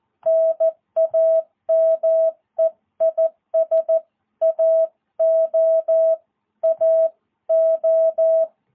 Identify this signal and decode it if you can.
Morse code, heard as a keyed tone, about 14 wpm, 660 Hz. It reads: NAME IS AO AO
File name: jh8cbh / 2022cw